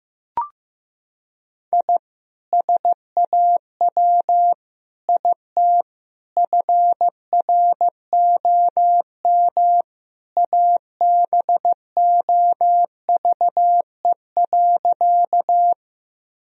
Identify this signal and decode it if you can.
Morse code, heard as a keyed tone, about 15 wpm, 700 Hz. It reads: I SAW IT FROM ABOVE.
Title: Sentences from Top 1000 Words - Review Part 2 - 15wpm